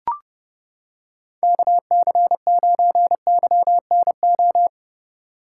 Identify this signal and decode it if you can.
Morse code, heard as a keyed tone, about 28 wpm, 700 Hz. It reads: KC9YNO